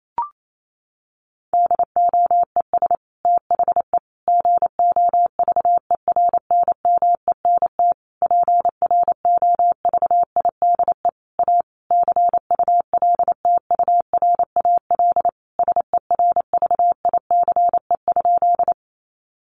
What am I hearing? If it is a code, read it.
Morse code, heard as a keyed tone, about 28 wpm, 700 Hz. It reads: DOES THE GOVERNMENT PROVIDE A CULTURAL SERVICE?
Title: Sentences from Top 1500 Words 28wpm